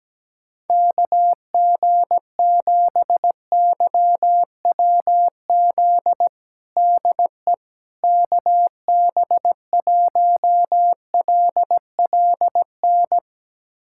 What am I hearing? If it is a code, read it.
Morse code, heard as a keyed tone, about 17 wpm, 700 Hz. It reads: KG7YWZ DE KB1LLN
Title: USA Callsign DE Callsign - 6 Characters - 17wpm